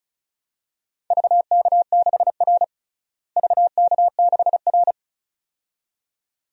Morse code, heard as a keyed tone, about 36 wpm, 700 Hz. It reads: VK6R VK6R
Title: DXCC-100 Call Signs - 4 Characters - Repeated 2x - 35wpm